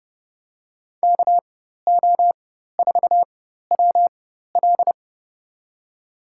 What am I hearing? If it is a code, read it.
Morse code, heard as a keyed tone, about 30 wpm, spaced as for 14 wpm, 700 Hz. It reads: KO4WL
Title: USA Call Signs - 5 Characters - 4x Character Spacing - 30wpm